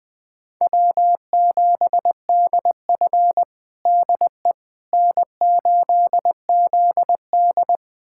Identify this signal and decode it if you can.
Morse code, heard as a keyed tone, about 20 wpm, 700 Hz. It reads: W7DF DE N8ZD
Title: USA Callsign DE Callsign - 4 Characters - 20wpm